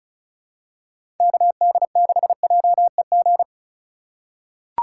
Morse code, heard as a keyed tone, about 35 wpm, 700 Hz. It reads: KD6JEG
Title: USA Call Signs - 6 Characters - 3x Character Spacing - 35wpm